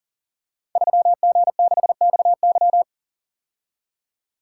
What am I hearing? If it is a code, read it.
Morse code, heard as a keyed tone, about 38 wpm, 690 Hz. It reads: 3G6XY
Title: Code Groups - 5 Letter:Numbers - 40wpm